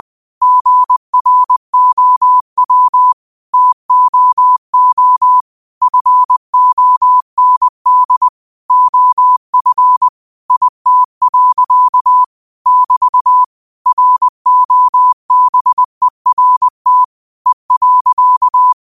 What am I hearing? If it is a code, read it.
Morse code, heard as a keyed tone, about 20 wpm, 1000 Hz. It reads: GROW TOO FOND OF IT. <BT> ROBERT E.
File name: qotdfiles / 20wpm